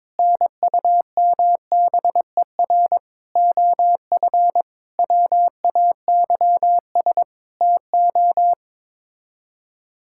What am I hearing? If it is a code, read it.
Morse code, heard as a keyed tone, about 22 wpm, 700 Hz. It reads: NUMBER OF WAYS TO
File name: practice / 22wpm / Sets of 7 Words - 22wpm